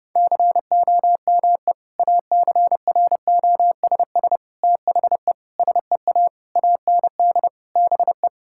Morse code, heard as a keyed tone, about 30 wpm, 700 Hz. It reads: COME ACROSS THE SEA AND BE
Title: Sentences from Top 1000 Words - Review Part 5 - 30wpm